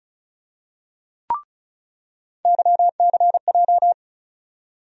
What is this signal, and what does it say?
Morse code, heard as a keyed tone, about 36 wpm, 700 Hz. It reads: YCJ
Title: USA Call Sign Postfixes - 3 Letters 35wpm